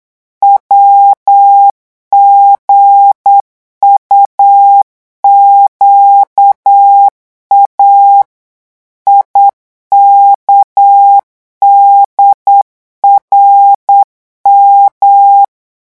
Morse code, heard as a keyed tone, about 8 wpm, 800 Hz. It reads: WGUQA IKDRM
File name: morse mp3 / revision8-vitesse 8 mots